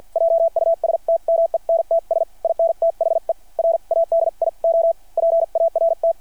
Morse code, heard as a keyed tone, about 49 wpm, spaced as for 42 wpm, 660 Hz. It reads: JUSTMENTS IN THE RADIO PART
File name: archived / CRSnet